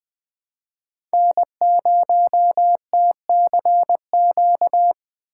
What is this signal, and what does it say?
Morse code, heard as a keyed tone, about 21 wpm, 700 Hz. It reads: N0TCQ
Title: Speed Racing - USA 1x3 Call Signs 17wpm